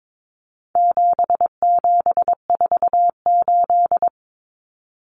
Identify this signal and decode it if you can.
Morse code, heard as a keyed tone, about 22 wpm, 700 Hz. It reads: 7748
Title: Common Cut Numbers - 4-Digit 22wpm